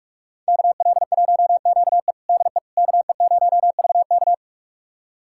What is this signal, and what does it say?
Morse code, heard as a keyed tone, about 46 wpm, 700 Hz. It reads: KL1QE DE KE0VK
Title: USA Callsign DE Callsign - 5 Characters - 45wpm